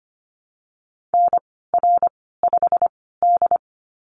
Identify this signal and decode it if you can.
Morse code, heard as a keyed tone, about 25 wpm, spaced as for 16 wpm, 700 Hz. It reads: NR5D
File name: Lesson 15g - Number 5 - Call Signs 25wpm